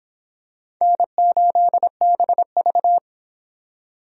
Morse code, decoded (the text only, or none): N8BV